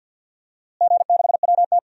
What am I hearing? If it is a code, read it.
Morse code, heard as a keyed tone, about 53 wpm, 700 Hz. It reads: G6WT
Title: DXCC-100 Call Signs - 4 Characters - Repeated 2x - 50wpm